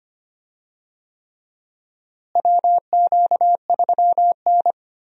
Morse code, heard as a keyed tone, about 25 wpm, 700 Hz. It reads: WQ3N
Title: USA Call Signs - 4 Characters - Repeated 3x - 25wpm